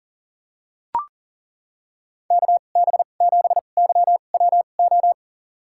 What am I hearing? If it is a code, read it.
Morse code, heard as a keyed tone, about 41 wpm, spaced as for 29 wpm, 690 Hz. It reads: KB7YWO
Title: USA Call Signs - 6 Characters - 2x Character Spacing - 40wpm